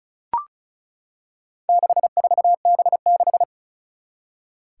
Morse code, heard as a keyed tone, about 36 wpm, 700 Hz. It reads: 64B6
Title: Four Challenging Letter-Number Combinations 35wpm